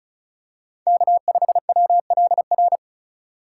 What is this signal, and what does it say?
Morse code, heard as a keyed tone, about 36 wpm, 700 Hz. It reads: K5WLR
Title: USA Call Signs - 5 Characters - 2x Character Spacing - 35wpm